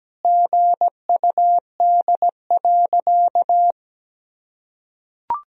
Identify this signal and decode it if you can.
Morse code, heard as a keyed tone, about 17 wpm, 700 Hz. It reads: GUD.
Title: Sentences from Top 200 Words and Abbreviations 17wpm